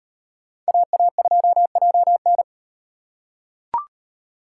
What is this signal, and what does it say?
Morse code, heard as a keyed tone, about 37 wpm, 700 Hz. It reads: AA2JN